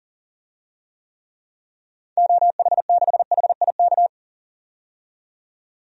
Morse code, heard as a keyed tone, about 40 wpm, 700 Hz. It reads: OH6HIK
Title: DXCC-100 Call Signs - 6 Characters - 40wpm